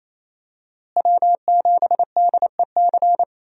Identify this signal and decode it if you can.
Morse code, heard as a keyed tone, about 27 wpm, 700 Hz. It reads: W7DEC
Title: USA Call Signs - 5 Characters - Repeated 2x - 28wpm